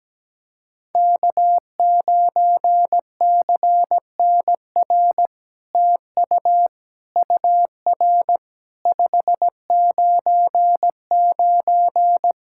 Morse code, heard as a keyed tone, about 17 wpm, 700 Hz. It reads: K9CNR TU UR 599
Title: SOTA QSO Practice 17wpm